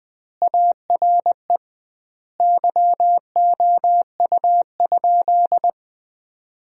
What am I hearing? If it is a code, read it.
Morse code, heard as a keyed tone, about 21 wpm, 700 Hz. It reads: ARE YOU?
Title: 2x Word-Spacing - Sentences from Top 100 Words - 20wpm